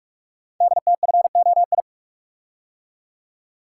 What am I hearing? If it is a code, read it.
Morse code, heard as a keyed tone, about 44 wpm, 690 Hz. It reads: DTFOI
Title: Code Groups - 5 Letters - 45wpm